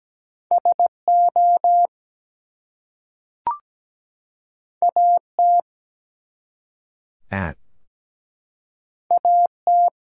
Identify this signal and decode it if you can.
Morse code, heard as a keyed tone, about 17 wpm, 700 Hz. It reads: SO AT AT
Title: Two Letter Words - 17wpm